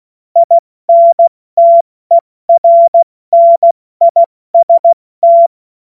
Morse code, heard as a keyed tone, about 16 wpm, spaced as for 14 wpm, 680 Hz. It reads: INTERNIST
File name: Morsekurs / Witze / Witz 16 April 2024 70 F80